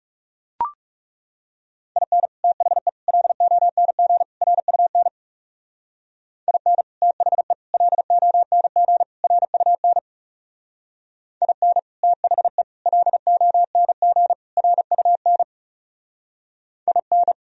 Morse code, heard as a keyed tone, about 37 wpm, 700 Hz. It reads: IN THE LONG RUN IN THE LONG RUN IN THE LONG RUN IN
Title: Speed-Racing - Sets of 4 Words 30wpm